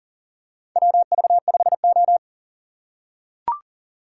Morse code, heard as a keyed tone, about 40 wpm, 700 Hz. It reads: WV5O
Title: USA Call Signs - 3x Character Spacing - 40wpm